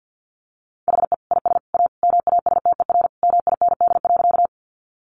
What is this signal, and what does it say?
Morse code, heard as a keyed tone, about 51 wpm, 700 Hz. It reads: HE IS A MASTER MIND.
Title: Sentences from Top 1300 Words 50wpm